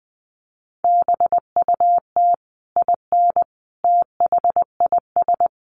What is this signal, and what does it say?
Morse code, heard as a keyed tone, about 19 wpm, 700 Hz. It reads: BUT IN THIS
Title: Sentences from Top 1200 Words 20wpm